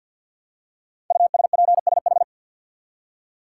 Morse code, heard as a keyed tone, about 52 wpm, 700 Hz. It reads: USPSH
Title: Code Groups - 5 Letters - 50wpm